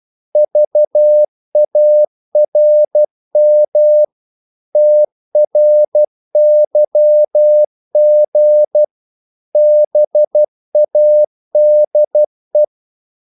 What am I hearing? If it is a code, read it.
Morse code, heard as a keyed tone, about 12 wpm, 600 Hz. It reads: VARM TRYG BADE
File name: Korte ord DK 12wpm